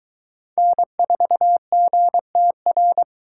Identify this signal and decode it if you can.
Morse code, heard as a keyed tone, about 22 wpm, 700 Hz. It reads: N4GTR